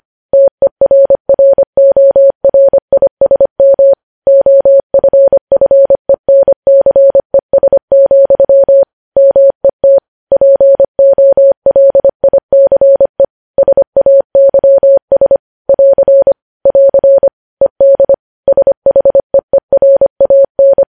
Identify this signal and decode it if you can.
Morse code, heard as a keyed tone, about 25 wpm, 550 Hz. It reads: TERRORISM OFFENCES, MET POLICE SAYS <AR> <AR> ED SHEERAN
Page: News Headlines in Morse Code at 25 WPM